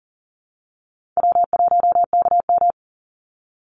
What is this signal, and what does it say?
Morse code, heard as a keyed tone, about 40 wpm, 700 Hz. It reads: W1KM
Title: Easy Call Signs Repeated 3x - 40wpm